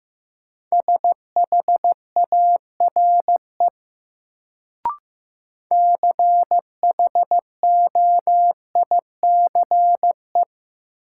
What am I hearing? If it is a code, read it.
Morse code, heard as a keyed tone, about 16 wpm, 700 Hz. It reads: SHARE CHOICE
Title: Sentences from Top 900 Words - New Word Drill - 15wpm